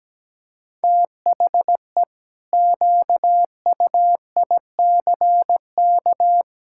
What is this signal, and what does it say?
Morse code, heard as a keyed tone, about 17 wpm, 700 Hz. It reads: THE QUICK
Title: Pangram Challenge v2 - 17wpm